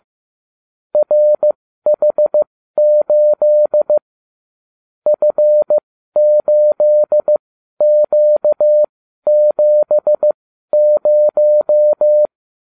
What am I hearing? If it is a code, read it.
Morse code, heard as a keyed tone, about 15 wpm, spaced as for 12 wpm, 600 Hz. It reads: RH8 F8Q70
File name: cw / ubacw35 3